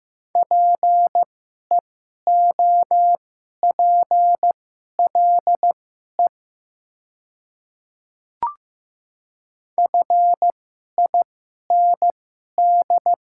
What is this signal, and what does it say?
Morse code, heard as a keyed tone, about 15 wpm, spaced as for 11 wpm, 700 Hz. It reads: PEOPLE FIND
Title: 2x Character Spacing - Top 100 Words - 15wpm